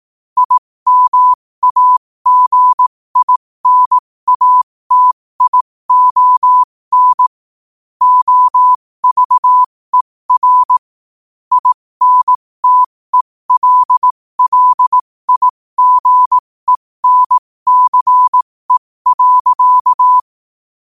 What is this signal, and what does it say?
Morse code, heard as a keyed tone, about 18 wpm, spaced as for 15 wpm, 1000 Hz. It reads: IMAGINATION OVER INTELLIGENCE.
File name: qotdfiles / 15wpm